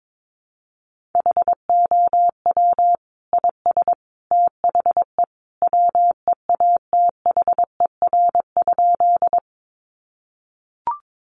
Morse code, heard as a keyed tone, about 22 wpm, 700 Hz. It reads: HOW IS THE WEATHER?
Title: Sentences from Top 800 Words 22wpm